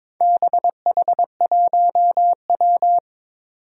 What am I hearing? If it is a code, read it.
Morse code, heard as a keyed tone, about 22 wpm, 700 Hz. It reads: BH1W
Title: Four Challenging Letter-Number Combinations 22wpm